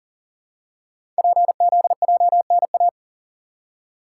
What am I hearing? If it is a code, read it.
Morse code, heard as a keyed tone, about 42 wpm, 700 Hz. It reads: PZJNA
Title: Code Groups - 5 Letters - 40wpm